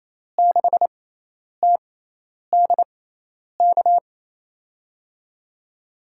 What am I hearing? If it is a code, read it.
Morse code, heard as a keyed tone, about 28 wpm, spaced as for 10 wpm, 700 Hz. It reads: 6TDK
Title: USA Call Signs - 5 Characters - 6x Character Spacing - 28wpm